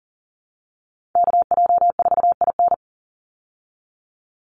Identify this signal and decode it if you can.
Morse code, heard as a keyed tone, about 40 wpm, 700 Hz. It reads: KJ4IN
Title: USA Call Signs 40wpm